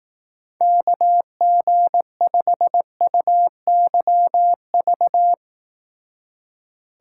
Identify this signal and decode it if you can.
Morse code, heard as a keyed tone, about 18 wpm, 700 Hz. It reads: KG5UYV
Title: Speed Racing - USA 2x3 Call Signs 15wpm